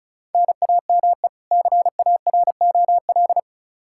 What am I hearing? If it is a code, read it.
Morse code, heard as a keyed tone, about 34 wpm, 700 Hz. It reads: NAME CAROL
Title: Practice QSO - Exchange 1 - Beginning - 35wpm